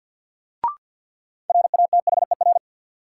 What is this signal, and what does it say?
Morse code, heard as a keyed tone, about 53 wpm, 700 Hz. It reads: RATHER
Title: Sentences from Top 700 Words - New Word Drill - 50wpm